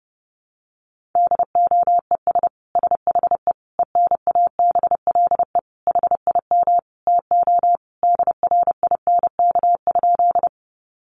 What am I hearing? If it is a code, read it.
Morse code, heard as a keyed tone, about 30 wpm, 700 Hz. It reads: DOES SHE ENABLE HIM TO DRINK?
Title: Sentences from Top 1500 Words 30wpm